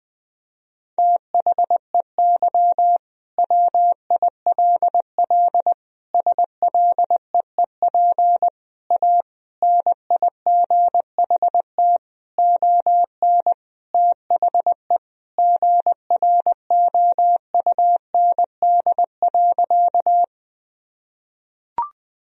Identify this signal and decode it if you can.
Morse code, heard as a keyed tone, about 21 wpm, 700 Hz. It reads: THEY WILL SLEEP A NIGHT ON THE GROUND.
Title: Sentences from Top 1000 Words - Review Part 2 - 20wpm